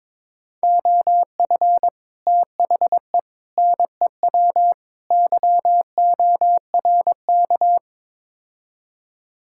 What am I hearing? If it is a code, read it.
Morse code, heard as a keyed tone, about 22 wpm, 700 Hz. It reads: OF THE NEW YORK
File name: practice / 22wpm / Build A Phrase - 5-Word Phrases - 22wpm